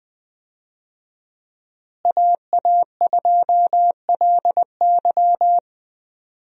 Morse code, heard as a keyed tone, about 21 wpm, 700 Hz. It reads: AA2LY
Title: USA Call Signs - 5 Characters - Repeated 3x - 20wpm